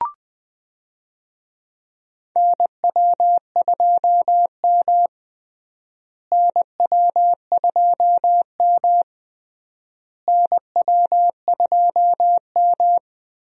Morse code, heard as a keyed tone, about 20 wpm, 700 Hz. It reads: NW2M NW2M NW2M
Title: USA Call Signs - 4 Characters - Repeated 3x - 20wpm